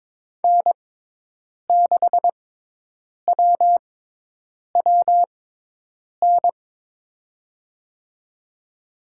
Morse code, heard as a keyed tone, about 22 wpm, spaced as for 8 wpm, 700 Hz. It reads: N6WWN
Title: USA Call Signs - 5 Characters - 6x Character Spacing - 22wpm